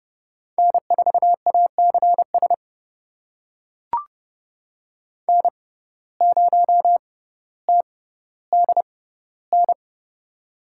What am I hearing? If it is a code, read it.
Morse code, heard as a keyed tone, about 31 wpm, spaced as for 15 wpm, 700 Hz. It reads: N4ACS N0TDN
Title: USA Call Signs - 6x Character Spacing - 30wpm